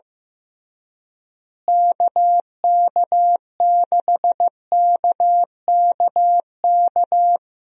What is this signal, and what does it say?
Morse code, heard as a keyed tone, about 15 wpm, 700 Hz. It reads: KK6KKK
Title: USA Call Signs - 6 Characters - Repeated 3x - 15wpm